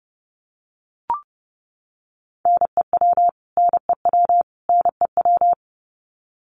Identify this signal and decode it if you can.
Morse code, heard as a keyed tone, about 30 wpm, 700 Hz. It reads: NEW NEW NEW
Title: Top 200 Words Repeated 30wpm